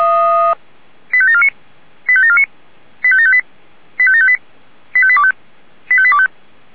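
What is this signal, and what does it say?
Five-tone selective call (ZVEI), heard as quick runs of five tones, tones of about 80 ms. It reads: 85749 85749 85758 85758 85825 85825
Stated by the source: ZVEI five-tone selective call received on SDR.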